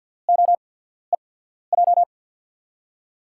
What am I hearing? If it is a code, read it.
Morse code, heard as a keyed tone, about 48 wpm, spaced as for 14 wpm, 700 Hz. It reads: OEJ
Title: USA Call Signs - 6 Characters - 8x Character Spacing - 50wpm